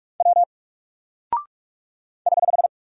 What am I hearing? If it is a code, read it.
Morse code, heard as a keyed tone, about 45 wpm, 700 Hz. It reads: W <HH>
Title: Single Character (Letter:Number:Punctuation:Prosign) Alt 45wpm